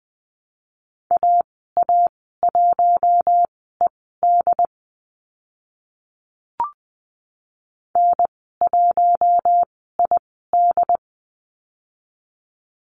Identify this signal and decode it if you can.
Morse code, heard as a keyed tone, about 20 wpm, spaced as for 14 wpm, 700 Hz. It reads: AA1ED N1ID
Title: 069 Lesson 12g - Letter d - Call Signs 20wpm